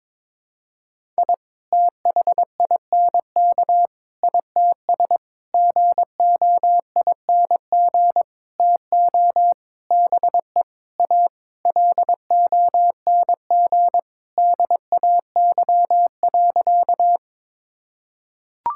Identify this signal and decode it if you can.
Morse code, heard as a keyed tone, about 22 wpm, 700 Hz. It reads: I THINK ITS GOING TO BE A LONG DAY.
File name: Sentences with 7-Word Phrases - 22wpm